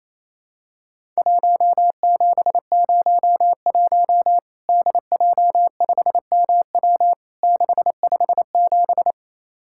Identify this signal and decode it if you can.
Morse code, heard as a keyed tone, about 28 wpm, 700 Hz. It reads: 1701 DJ5MW 657
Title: Worked All Europe DX Contest - QTC Exchange - 28wpm